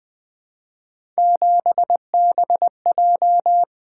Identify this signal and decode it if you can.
Morse code, heard as a keyed tone, about 20 wpm, 700 Hz. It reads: 7BJ
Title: Three Letter-Number Combinations - 20wpm